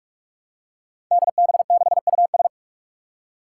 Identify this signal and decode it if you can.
Morse code, heard as a keyed tone, about 42 wpm, 700 Hz. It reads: DB6US
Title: Five Challenging Letter-Number Combinations 45wpm